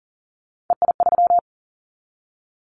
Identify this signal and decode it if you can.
Morse code, heard as a keyed tone, about 41 wpm, 700 Hz. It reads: EI3
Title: Callsign Prefixes for Top 50 DXCC - 40wpm